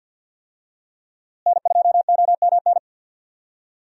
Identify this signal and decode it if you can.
Morse code, heard as a keyed tone, about 46 wpm, 700 Hz. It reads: N2OMN
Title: USA Call Signs - 5 Characters - Repeated 3x - 50wpm